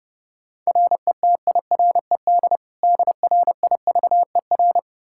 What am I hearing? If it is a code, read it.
Morse code, heard as a keyed tone, about 30 wpm, 700 Hz. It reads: RETIRED DRIVER
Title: Practice QSO - Exchange 3 - More Info - 30wpm